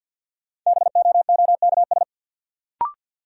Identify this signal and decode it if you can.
Morse code, heard as a keyed tone, about 47 wpm, 700 Hz. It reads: BOOKS
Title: Sentences from Top 1200 Words - New Word Drill Alt - 50wpm